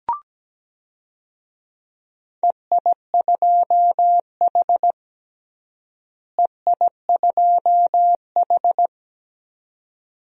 Morse code, heard as a keyed tone, about 17 wpm, 700 Hz. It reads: EI2H EI2H
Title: DXCC-100 Call Signs - 4 Characters - Repeated 2x - 17wpm